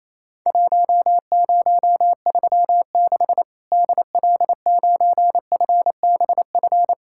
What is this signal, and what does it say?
Morse code, heard as a keyed tone, about 28 wpm, 700 Hz. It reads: 1036 DL9FBF
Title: Worked All Europe DX Contest - QTC Exchange - 28wpm